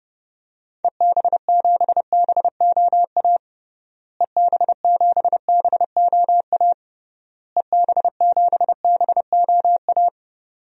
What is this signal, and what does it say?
Morse code, heard as a keyed tone, about 30 wpm, 700 Hz. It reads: EB7BOA EB7BOA EB7BOA
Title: DXCC-100 Call Signs - 6 Characters - Repeated 3x - 30wpm